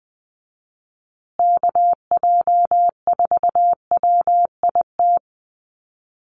Morse code, heard as a keyed tone, about 20 wpm, 700 Hz. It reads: KJ4WIT
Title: USA Call Signs Repeated 3x 20wpm